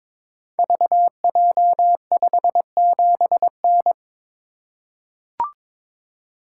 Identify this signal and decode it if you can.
Morse code, heard as a keyed tone, about 23 wpm, 700 Hz. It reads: VJ57N
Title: Code Groups - 5 Letter:Numbers - 22wpm